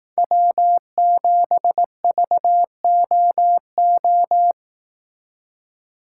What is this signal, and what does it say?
Morse code, heard as a keyed tone, about 18 wpm, 700 Hz. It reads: W7VOO
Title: Speed Racing - USA 1x3 Call Signs 15wpm